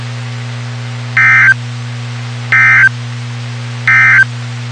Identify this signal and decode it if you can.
Emergency Alert System (SAME) tones, heard as harsh buzzing bursts. It is heard from 1.1 to 4.3 s.
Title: WXK91 NOAA Weather Radio